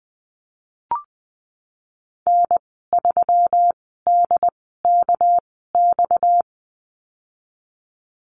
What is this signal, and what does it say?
Morse code, heard as a keyed tone, about 20 wpm, spaced as for 14 wpm, 700 Hz. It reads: N3DKX
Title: 231 Lesson 37g - Letter x - Call Signs 20wpm